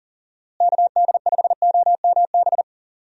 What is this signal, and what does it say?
Morse code, heard as a keyed tone, about 40 wpm, 700 Hz. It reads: KD5OMB
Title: USA Call Signs - 6 Characters - Repeated 3x - 40wpm